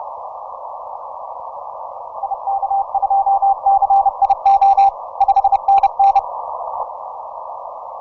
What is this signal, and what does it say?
Morse code, heard as a keyed tone, about 30 wpm, 790 Hz. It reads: IK2CIO 5NN
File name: simple-file-list / XR0YG 10m